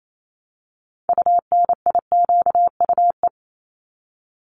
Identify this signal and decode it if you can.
Morse code, heard as a keyed tone, about 28 wpm, 700 Hz. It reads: UNIQUE
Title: Top 751-1000 Words 28wpm